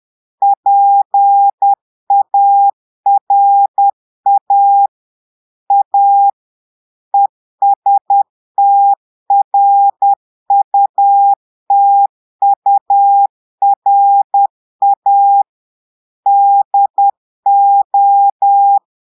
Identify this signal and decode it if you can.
Morse code, heard as a keyed tone, about 10 wpm, 800 Hz. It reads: PARA A ESTRUTURA DO
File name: portugais texte en clair 2 vitesse 10 mots minutes